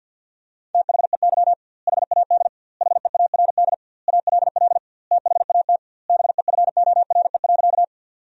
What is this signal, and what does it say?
Morse code, heard as a keyed tone, about 50 wpm, 700 Hz. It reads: THEY HAD HEARD ALL THAT BEFORE.
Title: 2x Word-Spacing - Sentences from Top 400 Words - 50wpm